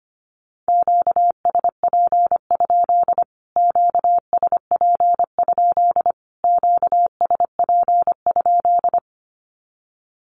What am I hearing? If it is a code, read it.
Morse code, heard as a keyed tone, about 25 wpm, 700 Hz. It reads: QSP? QSP? QSP?
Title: Q-Codes Repeated 3x 25wpm